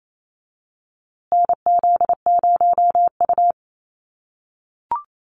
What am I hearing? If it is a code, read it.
Morse code, heard as a keyed tone, about 28 wpm, 700 Hz. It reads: NZ0U
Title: USA Call Signs Repeated 3x 28wpm